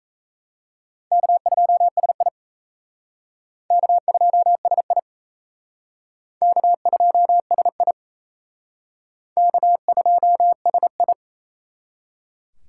Morse code, heard as a keyed tone, about 37 wpm, 700 Hz. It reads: K2SI K2SI K2SI K2SI